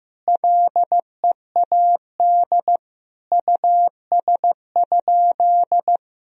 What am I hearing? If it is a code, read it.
Morse code, heard as a keyed tone, about 15 wpm, 700 Hz. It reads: LEAD US?
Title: Sentences from Top 1000 Words - Review Part 2 - 15wpm